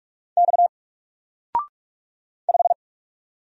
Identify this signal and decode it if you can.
Morse code, heard as a keyed tone, about 46 wpm, 700 Hz. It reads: X 5
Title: Single Letter-Number Alt 45wpm